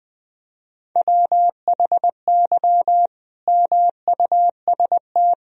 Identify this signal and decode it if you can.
Morse code, heard as a keyed tone, about 21 wpm, 700 Hz. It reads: WHY MUST
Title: Sentences from Top 1000 Words - Review Part 3 - 20wpm